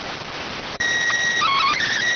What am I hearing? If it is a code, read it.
SSTV (slow-scan television), PD 290.
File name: findings / MR 2 Pel Sag T2 SSFSE 0009.gen-38 rx